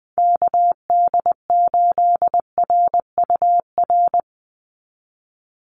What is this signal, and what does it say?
Morse code, heard as a keyed tone, about 20 wpm, 700 Hz. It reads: KD8RUR
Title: USA Call Signs 20wpm